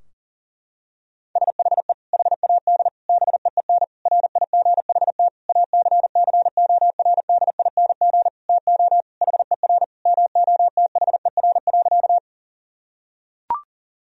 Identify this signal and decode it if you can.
Morse code, heard as a keyed tone, about 40 wpm, 700 Hz. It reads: SHE HAD BEEN RIGHT ACCORDING TO HER MOTHER.
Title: Sentences from Top 1000 Words - Review Part 4 - 40wpm